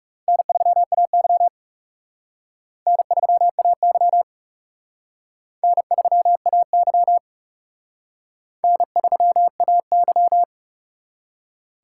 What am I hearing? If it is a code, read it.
Morse code, heard as a keyed tone, about 37 wpm, 700 Hz. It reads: N3AY N3AY N3AY N3AY
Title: Speed Racing - USA 1x2 Call Signs 30wpm